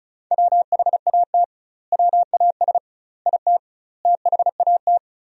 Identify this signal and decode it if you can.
Morse code, heard as a keyed tone, about 36 wpm, 700 Hz. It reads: WHAT WAS IT THAT
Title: 2x Word-Spacing - Sentences from Top 300 Words - 35wpm